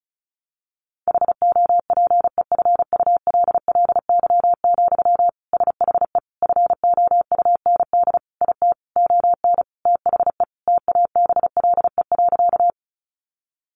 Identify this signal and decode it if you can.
Morse code, heard as a keyed tone, about 34 wpm, 700 Hz. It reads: HOPEFULLY, SHE FOUND IT ON THE TABLE.
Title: Sentences from Top 1500 Words 35wpm